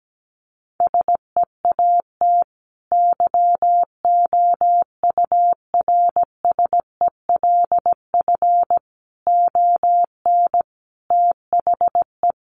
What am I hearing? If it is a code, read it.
Morse code, heard as a keyed tone, about 17 wpm, 700 Hz. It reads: SEAT YOURSELF ON THE
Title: Sentences from Top 1500 Words 17wpm